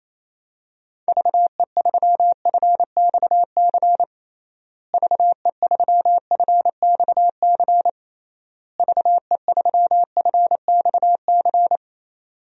Morse code, heard as a keyed tone, about 28 wpm, 700 Hz. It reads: VE3FXC VE3FXC VE3FXC
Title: DXCC-100 Call Signs - 6 Characters - Repeated 3x - 28wpm